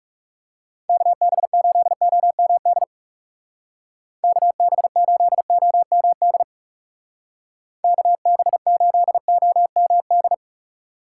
Morse code, heard as a keyed tone, about 36 wpm, 700 Hz. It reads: KB8OMD KB8OMD KB8OMD
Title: Speed Racing - USA 2x3 Call Signs 30wpm